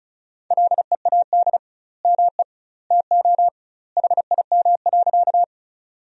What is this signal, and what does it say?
Morse code, heard as a keyed tone, about 34 wpm, 700 Hz. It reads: LEAD ME TO HIM.
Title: 2x Word-Spacing - Sentences from Top 500 Words - 35wpm